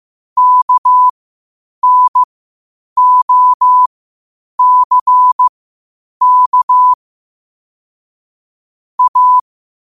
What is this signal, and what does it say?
Morse code, heard as a keyed tone, about 15 wpm, spaced as for 8 wpm, 1000 Hz. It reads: KNOCK A
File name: qotdfiles / 8wpm